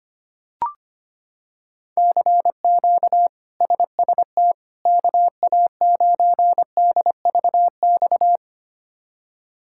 Morse code, heard as a keyed tone, about 25 wpm, 700 Hz. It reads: CQ SST KA9DVX
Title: K1USN SST QSO Practice - 25wpm